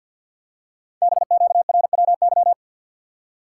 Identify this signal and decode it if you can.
Morse code, heard as a keyed tone, about 50 wpm, 700 Hz. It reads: BQRWY
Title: Code Groups - 5 Letters - 50wpm